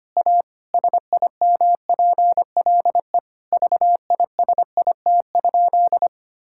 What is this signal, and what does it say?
Morse code, heard as a keyed tone, about 25 wpm, 700 Hz. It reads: A SIMPLE VISIT?
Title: Sentences from Top 1000 Words - Review Part 6 - 25wpm